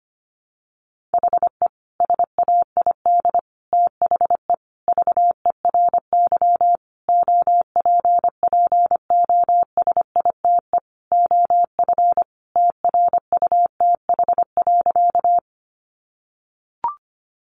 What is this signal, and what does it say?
Morse code, heard as a keyed tone, about 25 wpm, 700 Hz. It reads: HE SAID THE VERY OPPOSITE OF TRUTH.
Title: Sentences from Top 1300 Words 25wpm